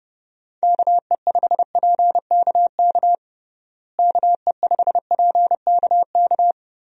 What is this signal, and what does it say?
Morse code, heard as a keyed tone, about 30 wpm, 700 Hz. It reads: KE5PKK KE5PKK
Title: USA Call Signs - 6 Characters - Repeated 2x - 30wpm